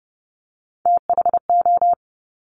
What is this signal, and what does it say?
Morse code, heard as a keyed tone, about 30 wpm, 690 Hz. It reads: THO
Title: Top Three Letter Combinations in Words 30wpm